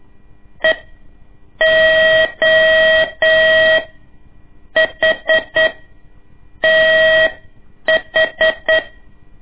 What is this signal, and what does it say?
Morse code, heard as a keyed tone, about 8 wpm, spaced as for 6 wpm, 640 Hz. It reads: EOHTH
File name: ik1weg2 / CORSI CW / lez2-1